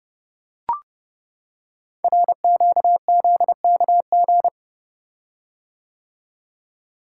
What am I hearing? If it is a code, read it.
Morse code, heard as a keyed tone, about 28 wpm, 700 Hz. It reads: RQZKG
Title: Code Groups - 5 Letter:Numbers - 30wpm